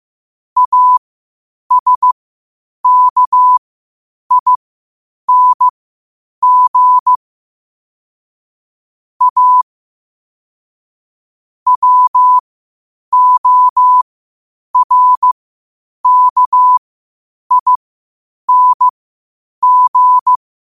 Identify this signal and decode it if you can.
Morse code, heard as a keyed tone, about 15 wpm, spaced as for 8 wpm, 1000 Hz. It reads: ASKING A WORKING
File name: qotdfiles / 8wpm